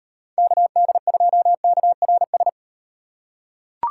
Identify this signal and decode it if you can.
Morse code, heard as a keyed tone, about 36 wpm, 700 Hz. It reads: KD2KRS